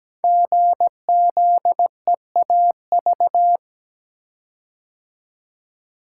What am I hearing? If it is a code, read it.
Morse code, heard as a keyed tone, about 17 wpm, 700 Hz. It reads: GZEAV
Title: Code Groups - 5 Letter:Numbers - 17wpm